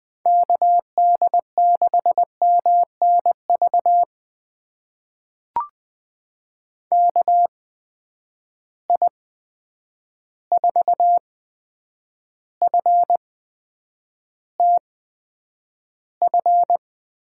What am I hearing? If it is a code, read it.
Morse code, heard as a keyed tone, about 20 wpm, spaced as for 9 wpm, 700 Hz. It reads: KD6MNV KI4FTF
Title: USA Call Signs - 6 Characters - 8x Character Spacing - 20wpm